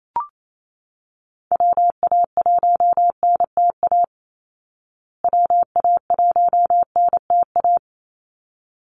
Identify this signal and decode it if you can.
Morse code, heard as a keyed tone, about 28 wpm, 700 Hz. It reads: WA1NTA WA1NTA
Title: Call Signs Repeated 3x - 28wpm